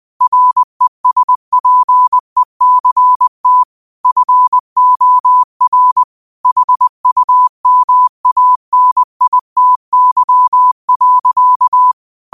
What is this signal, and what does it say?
Morse code, heard as a keyed tone, about 20 wpm, 1000 Hz. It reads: RESPECT FOR HUMANITY.